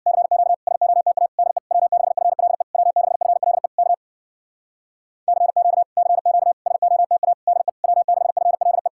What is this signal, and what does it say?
Morse code, heard as a keyed tone, about 56 wpm, 700 Hz. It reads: CQ SOTA DE W6FLE W6FLE K CQ CQ SOTA DE W6FLE
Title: Speed-Racing Common QSO Phrases - 50wpm